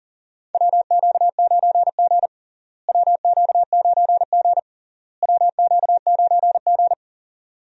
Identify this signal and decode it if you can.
Morse code, heard as a keyed tone, about 40 wpm, 700 Hz. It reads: WQ9G WQ9G WQ9G
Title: USA Call Signs - 4 Characters - Repeated 3x - 40wpm